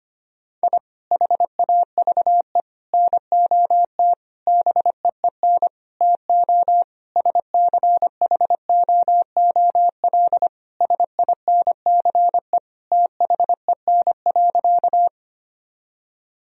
Morse code, heard as a keyed tone, about 25 wpm, 700 Hz. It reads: I HAVE NOT BEEN TO SCHOOL SINCE THEN.
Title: Sentences from Top 1000 Words - Review Part 1 - 25wpm